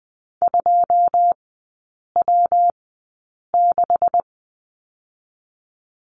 Morse code, heard as a keyed tone, about 20 wpm, spaced as for 8 wpm, 700 Hz. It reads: 2W6
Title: 191 Lesson 31e - Letter v - Three Characters 20wpm